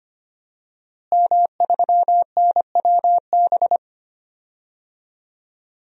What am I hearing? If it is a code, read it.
Morse code, heard as a keyed tone, about 25 wpm, 700 Hz. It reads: M3NWB
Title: DXCC-100 Call Signs - 5 Characters - Repeated 3x - 25wpm